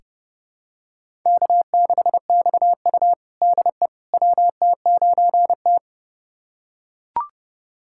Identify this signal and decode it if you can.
Morse code, heard as a keyed tone, about 30 wpm, 700 Hz. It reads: K6XU DE WT9T
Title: USA Callsign DE Callsign - 4 Characters - 30wpm